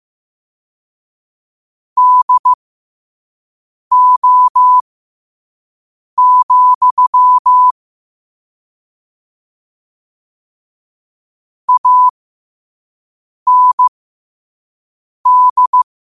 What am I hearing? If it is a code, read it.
Morse code, heard as a keyed tone, about 15 wpm, spaced as for 5 wpm, 1000 Hz. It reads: DO, AND
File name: qotdfiles / 5wpm